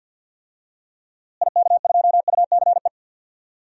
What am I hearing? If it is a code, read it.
Morse code, heard as a keyed tone, about 48 wpm, 700 Hz. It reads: IK2UCE
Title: DXCC-100 Call Signs - 6 Characters - Repeated 3x - 50wpm